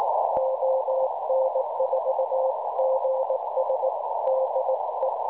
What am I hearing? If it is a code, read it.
Morse code, heard as a keyed tone, about 18 wpm, 530 Hz, with noise. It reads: ON4GS DE
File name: cwronde / websdr recording .6kHz